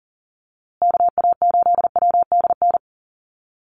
Morse code, heard as a keyed tone, about 40 wpm, 700 Hz. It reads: KA8WDN
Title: USA Call Signs Repeated 3x 40wpm